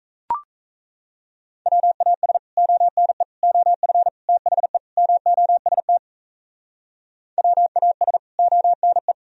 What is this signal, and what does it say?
Morse code, heard as a keyed tone, about 37 wpm, 700 Hz. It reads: WAS ONE OF THE MOST WAS ONE